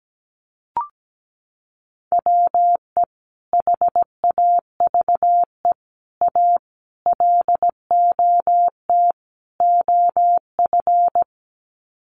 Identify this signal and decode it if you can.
Morse code, heard as a keyed tone, about 17 wpm, 700 Hz. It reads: WE HAVE A LOT OF
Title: Sets of 5 Words 17wpm